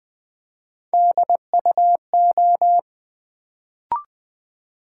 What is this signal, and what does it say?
Morse code, heard as a keyed tone, about 20 wpm, 700 Hz. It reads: DUO
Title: Build A Word - 1 Letter At A Time - 3-Letter Words - 20wpm